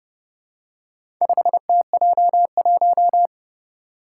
Morse code, heard as a keyed tone, about 29 wpm, 700 Hz. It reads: 5TJ1
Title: Four Challenging Letter-Number Combinations 30wpm